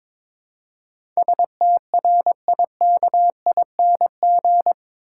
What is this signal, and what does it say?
Morse code, heard as a keyed tone, about 22 wpm, 700 Hz. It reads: STRIKING
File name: practice / 22wpm / Eight Letter Words - 22wpm